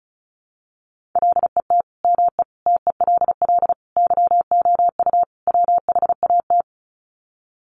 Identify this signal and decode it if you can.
Morse code, heard as a keyed tone, about 34 wpm, 700 Hz. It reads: LET ME TELL YOU WHAT
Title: Sets of 5 Words Encore 35wpm